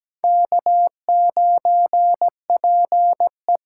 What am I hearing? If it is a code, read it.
Morse code, heard as a keyed tone, about 17 wpm, 700 Hz. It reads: K9PE
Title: USA Call Signs - 5 Characters - 8x Character Spacing - 17wpm